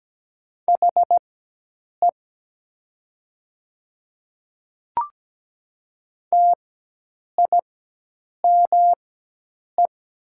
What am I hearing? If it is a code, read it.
Morse code, heard as a keyed tone, about 17 wpm, spaced as for 8 wpm, 700 Hz. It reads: HE TIME
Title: 4x Character Spacing - Top 100 Words - 17wpm